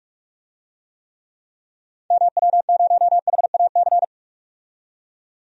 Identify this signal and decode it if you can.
Morse code, heard as a keyed tone, about 46 wpm, 700 Hz. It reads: MW0HAC
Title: DXCC-100 Call Signs - 6 Characters - 45wpm